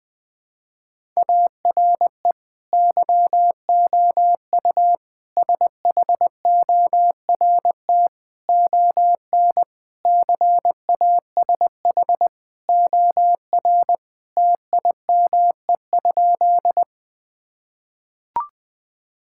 Morse code, heard as a keyed tone, about 20 wpm, 700 Hz. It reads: ARE YOU SHORT ON CASH OR TIME?
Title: Sentences from Top 1000 Words - Review Part 5 - 20wpm